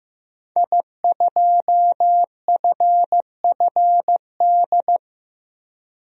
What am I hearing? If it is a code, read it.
Morse code, heard as a keyed tone, about 15 wpm, 700 Hz. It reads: I2FFD
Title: DXCC-100 Call Signs - 5 Characters - 15wpm